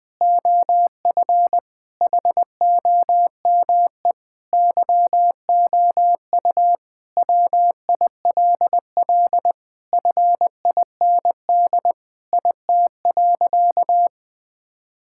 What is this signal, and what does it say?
Morse code, heard as a keyed tone, about 20 wpm, 700 Hz. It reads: OF HOME YOU WILL FIND IT.
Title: Sentences with 4-Word Phrases - 20wpm